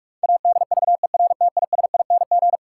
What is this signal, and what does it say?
Morse code, heard as a keyed tone, about 41 wpm, 700 Hz. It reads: ADVERTISING
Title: Sentences from Top 1000 Words - Review Part 6 - New Word Drill - 45wpm